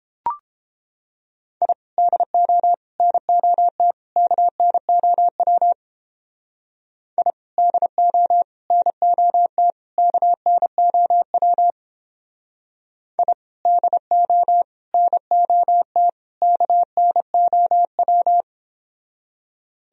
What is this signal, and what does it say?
Morse code, heard as a keyed tone, about 30 wpm, 700 Hz. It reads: I DO NOT KNOW I DO NOT KNOW I DO NOT KNOW